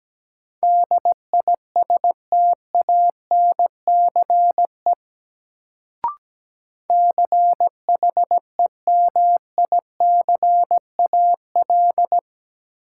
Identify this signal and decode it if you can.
Morse code, heard as a keyed tone, about 17 wpm, 700 Hz. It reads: DISTANCE CHEMICAL
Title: Sentences from Top 1200 Words - New Word Drill Alt - 17wpm